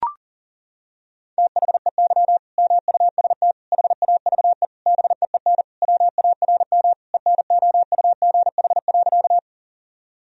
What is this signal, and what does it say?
Morse code, heard as a keyed tone, about 40 wpm, 700 Hz. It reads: THEY MUST HAVE BEEN WARM ENOUGH.
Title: Sentences from Top 1000 Words - Review Part 2 - 40wpm